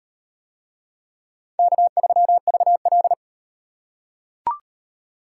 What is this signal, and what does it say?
Morse code, heard as a keyed tone, about 37 wpm, 700 Hz. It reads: K3VL